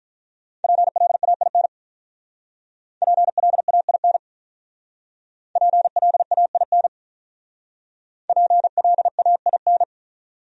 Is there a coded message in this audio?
Yes: Morse code from 0.6 to 9.8 s, heard as a keyed tone, about 47 wpm, 700 Hz. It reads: PLAIN PLAIN PLAIN PLAIN